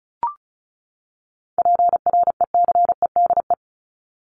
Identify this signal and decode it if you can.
Morse code, heard as a keyed tone, about 34 wpm, 700 Hz. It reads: PRECEDE
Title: ING Suffix - Top 100 Words with Base Words 35wpm